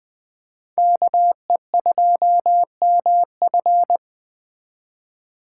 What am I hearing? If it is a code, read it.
Morse code, heard as a keyed tone, about 20 wpm, 700 Hz. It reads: KE2MF
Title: Speed Racing - USA 2x2 Call Signs 17wpm